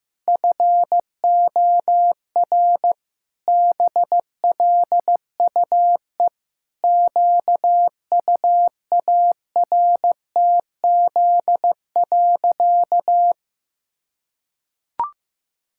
Morse code, heard as a keyed tone, about 15 wpm, 700 Hz. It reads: FOR BLUE QUARTZ.
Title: Pangram Challenge v2 - 15wpm